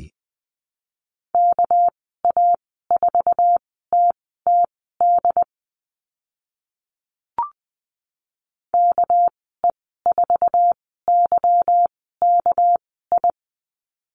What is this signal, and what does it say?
Morse code, heard as a keyed tone, about 20 wpm, spaced as for 14 wpm, 700 Hz. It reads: KA4TTD KE4YKI
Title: 213 Lesson 34g - Number 8 - Call Signs 20wpm